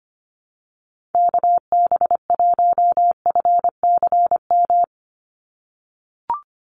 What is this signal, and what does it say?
Morse code, heard as a keyed tone, about 25 wpm, 700 Hz. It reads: KB1FCM
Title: USA Call Signs 25wpm